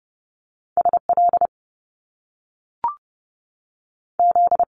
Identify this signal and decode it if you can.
Morse code, heard as a keyed tone, about 30 wpm, 700 Hz. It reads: SL Z
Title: Top Two Letter Combinations in Words 30wpm